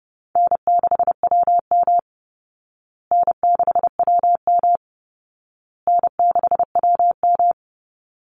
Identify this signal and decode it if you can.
Morse code, heard as a keyed tone, about 28 wpm, 700 Hz. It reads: N6WM N6WM N6WM
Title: Call Signs Repeated 2x - 30wpm